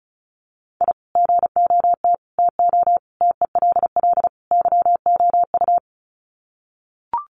Morse code, heard as a keyed tone, about 34 wpm, 700 Hz. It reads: I GOT TO TELL YOU
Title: Sets of 5 Words Encore2 35wpm